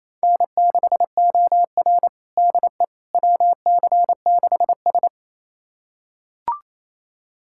Morse code, heard as a keyed tone, about 28 wpm, 700 Hz. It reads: N6OR DE WC6S
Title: USA Callsign DE Callsign - 4 Characters - 28wpm